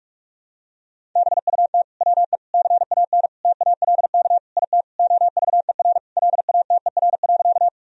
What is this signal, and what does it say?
Morse code, heard as a keyed tone, about 48 wpm, 700 Hz. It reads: BUT WE CAN TALK IT OVER LATER.
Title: Sentences from Top 1000 Words - Review Part 4 - 45wpm